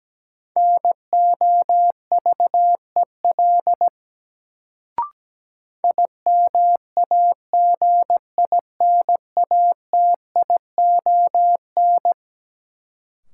Morse code, heard as a keyed tone, about 17 wpm, 700 Hz. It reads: NOVEL IMAGINATION
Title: Sentences from Top 1300 Words - New Word Drill Alt - 17wpm